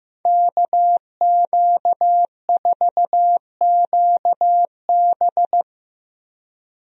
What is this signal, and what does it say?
Morse code, heard as a keyed tone, about 15 wpm, 700 Hz. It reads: KQ4QB
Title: USA Call Signs - 5 Characters - Repeated 2x - 15wpm